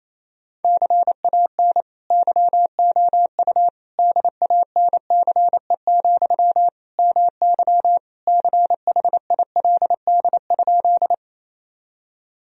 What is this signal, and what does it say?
Morse code, heard as a keyed tone, about 28 wpm, 700 Hz. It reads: CAN YOU DANCE, MY CHILD?
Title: Sentences from Top 1000 Words - Review Part 4 - 28wpm